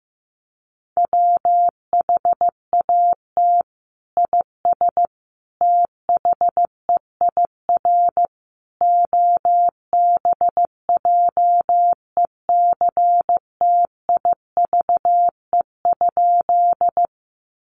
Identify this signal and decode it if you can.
Morse code, heard as a keyed tone, about 15 wpm, 700 Hz. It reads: WHAT IS THEIR OBJECTIVE?
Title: Sentences from Top 1200 Words 15wpm